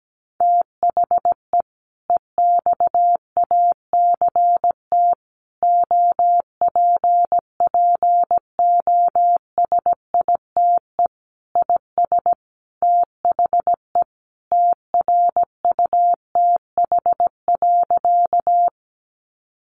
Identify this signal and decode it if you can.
Morse code, heard as a keyed tone, about 17 wpm, 700 Hz. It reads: THE EXACT OPPOSITE IS THE TRUTH.
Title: Sentences from Top 1300 Words 17wpm